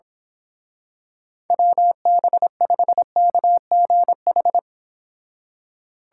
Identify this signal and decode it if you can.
Morse code, heard as a keyed tone, about 26 wpm, 700 Hz. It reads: WB5KGH
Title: Speed Racing - USA 2x3 Call Signs 22wpm